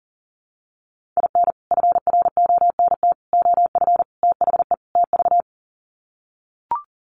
Morse code, heard as a keyed tone, about 40 wpm, 700 Hz. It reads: IN FRONT OF THE TV
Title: Sets of 5 Words Encore2 40wpm